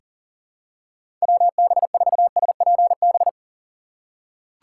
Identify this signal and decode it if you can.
Morse code, heard as a keyed tone, about 40 wpm, 700 Hz. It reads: WB4SPB
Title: K1USN SST QSO Practice - 40wpm